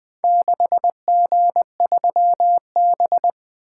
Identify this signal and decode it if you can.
Morse code, heard as a keyed tone, about 23 wpm, 700 Hz. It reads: 6G3B
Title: Speed-Racing - Four Letter-Number Combinations - 20wpm